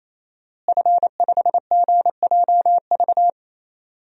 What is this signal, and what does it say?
Morse code, heard as a keyed tone, about 27 wpm, 700 Hz. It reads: F5GJV
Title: DXCC-100 Call Signs - 5 Characters - Repeated 3x - 28wpm